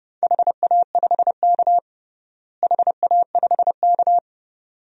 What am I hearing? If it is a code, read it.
Morse code, heard as a keyed tone, about 30 wpm, 700 Hz. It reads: HA5K HA5K
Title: DXCC-100 Call Signs - 4 Characters - Repeated 2x - 30wpm